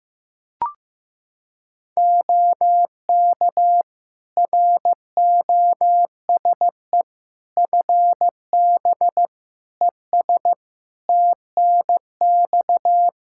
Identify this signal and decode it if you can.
Morse code, heard as a keyed tone, about 15 wpm, 700 Hz. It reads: OK ROSE FB ES TNX
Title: Practice QSO - Exchange 2 - Info - 15wpm